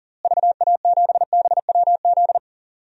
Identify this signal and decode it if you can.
Morse code, heard as a keyed tone, about 40 wpm, 700 Hz. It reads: VA7BWZ
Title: DXCC-100 Call Signs - 6 Characters - Repeated 2x - 40wpm